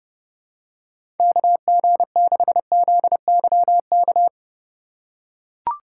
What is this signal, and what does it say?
Morse code, heard as a keyed tone, about 32 wpm, 700 Hz. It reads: KG6ZYK
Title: USA Call Signs - 3x Character Spacing - 30wpm